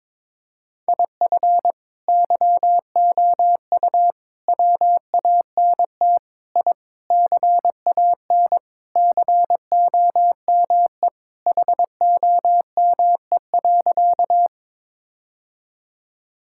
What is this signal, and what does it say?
Morse code, heard as a keyed tone, about 22 wpm, 700 Hz. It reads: IF YOU WANT I CAN COME HOME.
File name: practice / 22wpm / Sentences with 3-Word Phrases - 22wpm